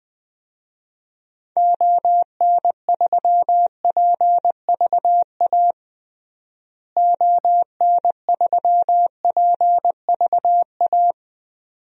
Morse code, heard as a keyed tone, about 20 wpm, 700 Hz. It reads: ON3PVA ON3PVA
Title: DXCC-100 Call Signs - 6 Characters - Repeated 2x - 20wpm